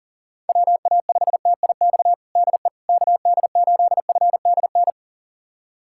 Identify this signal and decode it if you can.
Morse code, heard as a keyed tone, about 37 wpm, 700 Hz. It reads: WA5TIX DE KD8FDN
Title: USA Callsign DE Callsign - 6 Characters - 40wpm